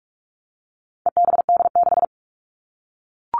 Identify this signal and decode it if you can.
Morse code, heard as a keyed tone, about 44 wpm, 700 Hz. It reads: EBD6
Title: All Cut Numbers - 4-Digit 45wpm